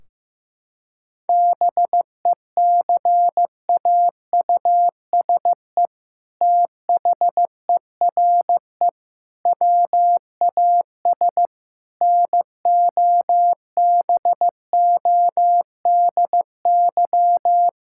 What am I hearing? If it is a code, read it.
Morse code, heard as a keyed tone, about 15 wpm, 700 Hz. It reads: BECAUSE THERE WAS NOBODY
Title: Sentences from Top 1600 Words 15wpm